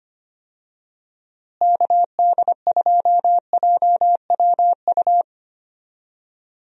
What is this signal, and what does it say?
Morse code, heard as a keyed tone, about 25 wpm, 700 Hz. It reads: KD2JWU
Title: USA Call Signs - 6 Characters - Repeated 3x - 25wpm